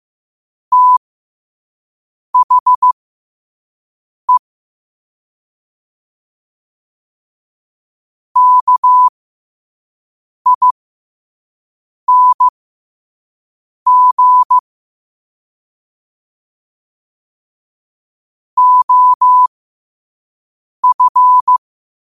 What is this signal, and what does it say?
Morse code, heard as a keyed tone, about 15 wpm, spaced as for 5 wpm, 1000 Hz. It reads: THE KING OF
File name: qotdfiles / 5wpm